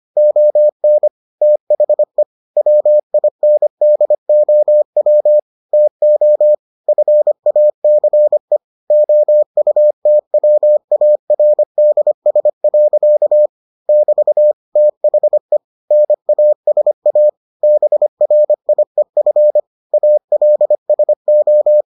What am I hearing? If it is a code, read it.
Morse code, heard as a keyed tone, about 25 wpm, 600 Hz. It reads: ON THE WINDOW TO FACE OUTWARDS. <BT> THE NASA BRIEF ALSO